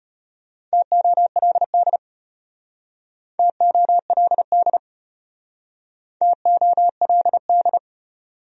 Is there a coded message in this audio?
Yes: Morse code from 0.7 to 7.8 s, heard as a keyed tone, about 34 wpm, 700 Hz. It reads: TOLD TOLD TOLD